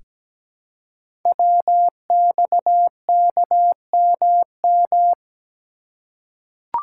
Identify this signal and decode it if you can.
Morse code, heard as a keyed tone, about 17 wpm, 700 Hz. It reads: WXKMM
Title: Code Groups - 5 Letters - 17wpm